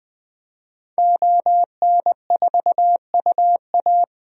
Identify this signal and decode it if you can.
Morse code, heard as a keyed tone, about 20 wpm, 700 Hz. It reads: ON4UA
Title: DXCC-100 Call Signs - 5 Characters - 20wpm